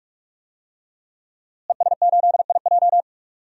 Morse code, heard as a keyed tone, about 46 wpm, 700 Hz. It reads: ES8IJ
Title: DXCC-100 Call Signs - 5 Characters - Repeated 2x - 45wpm